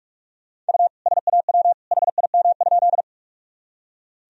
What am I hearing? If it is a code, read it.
Morse code, heard as a keyed tone, about 46 wpm, 700 Hz. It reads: U SAW HIM?
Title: Sentences from Top 300 Words and Abbreviations 45wpm